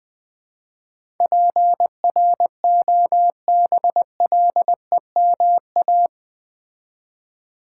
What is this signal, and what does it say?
Morse code, heard as a keyed tone, about 20 wpm, 700 Hz. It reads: PROBLEMA
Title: Build A Word - 4 Letters At A Time - 11-Letter Words - 20wpm